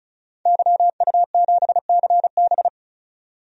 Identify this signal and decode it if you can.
Morse code, heard as a keyed tone, about 34 wpm, 700 Hz. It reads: YU7CB
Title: DXCC-100 Call Signs - 5 Characters - 35wpm